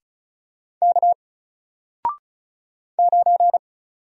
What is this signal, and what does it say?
Morse code, heard as a keyed tone, about 35 wpm, 700 Hz. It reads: K 9
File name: Single-Letter-Number-35wpm